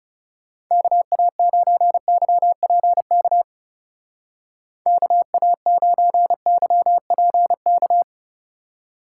Morse code, heard as a keyed tone, about 37 wpm, 700 Hz. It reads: KA9YPK KA9YPK
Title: Speed Racing - USA 2x3 Call Signs 30wpm